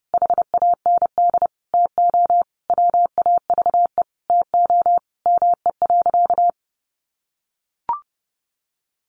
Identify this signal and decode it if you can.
Morse code, heard as a keyed tone, about 30 wpm, 700 Hz. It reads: HAND TO WAVE TO ME.
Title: Sentences from Top 1400 Words 30wpm